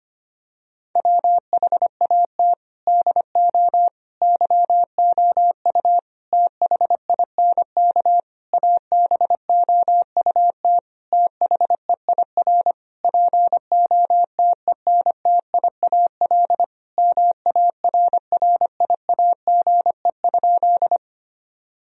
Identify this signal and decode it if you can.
Morse code, heard as a keyed tone, about 25 wpm, 700 Hz. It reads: WHAT DO YOU THINK ABOUT THEIR POTENTIAL MARRIAGE?
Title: Sentences from Top 1000 Words - Review Part 4 - 25wpm